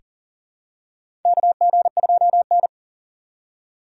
Morse code, heard as a keyed tone, about 40 wpm, 700 Hz. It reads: KG2N
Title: USA Call Signs - 4x Character Spacing - 40wpm